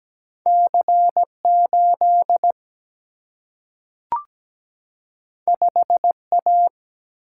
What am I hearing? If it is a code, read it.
Morse code, heard as a keyed tone, about 17 wpm, 700 Hz. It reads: C8 5A
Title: Single-Letter Single-Number 17wpm